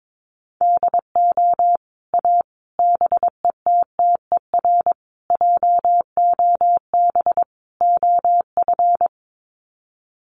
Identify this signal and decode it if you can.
Morse code, heard as a keyed tone, about 22 wpm, 700 Hz. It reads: DO A BETTER JOB OF
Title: Sets of 5 Words Encore2 22wpm